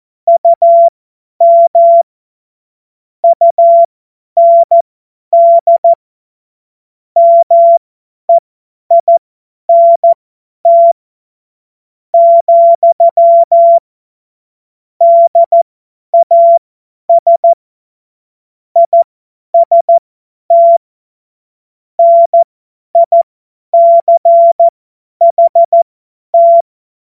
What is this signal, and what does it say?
Morse code, heard as a keyed tone, about 14 wpm, spaced as for 10 wpm, 680 Hz. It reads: UM UND MEINT, DAS IST NICHT
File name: Witz-09-September-2025-50-F70